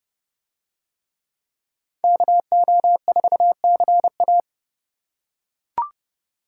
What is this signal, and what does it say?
Morse code, heard as a keyed tone, about 28 wpm, 700 Hz. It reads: KO4CA